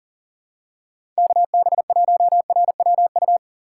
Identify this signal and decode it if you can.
Morse code, heard as a keyed tone, about 40 wpm, 700 Hz. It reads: KB1RWU
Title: USA Call Signs - 6 Characters - 2x Character Spacing - 40wpm